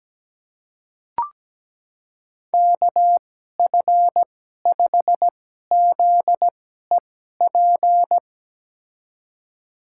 Morse code, heard as a keyed tone, about 17 wpm, spaced as for 12 wpm, 700 Hz. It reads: KF5ZEP
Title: USA Call Signs - 6 Characters - 2x Character Spacing - 17wpm